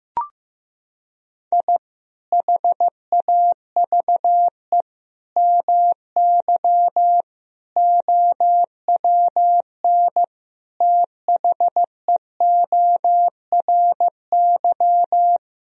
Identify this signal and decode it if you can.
Morse code, heard as a keyed tone, about 16 wpm, 700 Hz. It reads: I HAVE MY OWN THEORY
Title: Sentences from Top 1000 Words - Review Part 3 - 15wpm